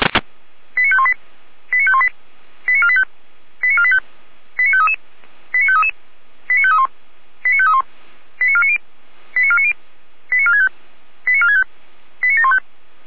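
Five-tone selective call (ZVEI), heard as quick runs of five tones, tones of about 70 ms. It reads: 89518 89518 89485 89485 89533 89533 89531 89531 89490 89490 89465 89465 89715
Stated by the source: Five-tone ZVEI selective call received on SDR.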